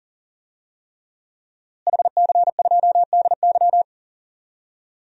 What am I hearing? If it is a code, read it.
Morse code, heard as a keyed tone, about 40 wpm, 700 Hz. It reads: HC2DY
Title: DXCC-100 Call Signs - 5 Characters - Repeated 3x - 40wpm